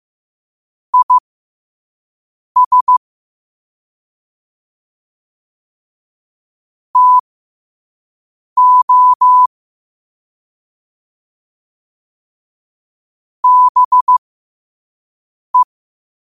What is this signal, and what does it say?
Morse code, heard as a keyed tone, about 15 wpm, spaced as for 5 wpm, 1000 Hz. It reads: IS TO BE
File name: 5wpm